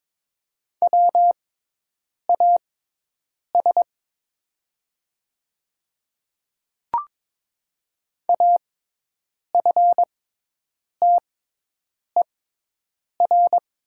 Morse code, heard as a keyed tone, about 22 wpm, spaced as for 8 wpm, 700 Hz. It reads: WAS AFTER
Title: 6x Character Spacing - Top 200 Words - 22wpm